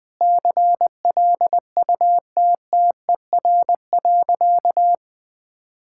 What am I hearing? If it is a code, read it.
Morse code, heard as a keyed tone, about 20 wpm, 700 Hz. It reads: CLUTTER.
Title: Sentences with 5-Word Phrases - 20wpm